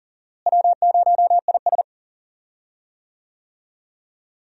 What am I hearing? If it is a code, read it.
Morse code, heard as a keyed tone, about 40 wpm, 700 Hz. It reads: W0IS
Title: USA Call Signs - 4 Characters - Repeated 3x - 40wpm